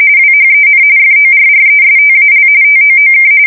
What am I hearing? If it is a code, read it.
RTTY, about 45.45 baud. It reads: WHEN FORRESTAL TOOK
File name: fsk170 / usnavy5